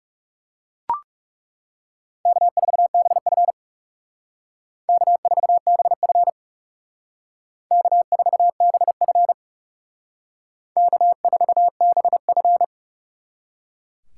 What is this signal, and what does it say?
Morse code, heard as a keyed tone, about 38 wpm, 700 Hz. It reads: K4BF K4BF K4BF K4BF